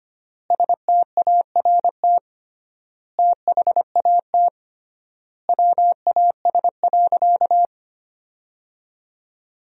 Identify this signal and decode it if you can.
Morse code, heard as a keyed tone, about 25 wpm, 700 Hz. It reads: START THAT WAS.
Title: 3x Word-Spacing - Sentences from Top 300 Words - 25wpm